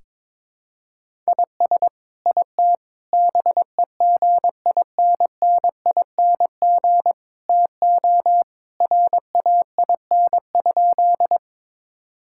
Morse code, heard as a keyed tone, about 22 wpm, 700 Hz. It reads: IS IT BEGINNING TO RAIN?
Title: Sentences from Top 1000 Words - Review Part 5 - 22wpm